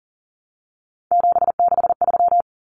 Morse code, heard as a keyed tone, about 40 wpm, 700 Hz. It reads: 763
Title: Numbers - 3-Digit 40wpm